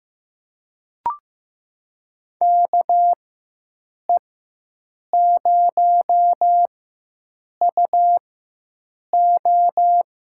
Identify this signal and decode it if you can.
Morse code, heard as a keyed tone, about 15 wpm, spaced as for 7 wpm, 700 Hz. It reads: KE0UO
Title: USA Call Signs - 6 Characters - 4x Character Spacing - 15wpm